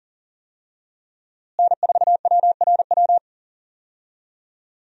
Morse code, heard as a keyed tone, about 40 wpm, 700 Hz. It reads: N4WRW
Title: USA Call Signs - 5 Characters - Repeated 3x - 40wpm